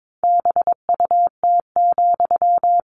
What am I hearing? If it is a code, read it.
Morse code, heard as a keyed tone, about 22 wpm, 700 Hz. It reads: BUT,
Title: Sentences from Top 600 Words 22wpm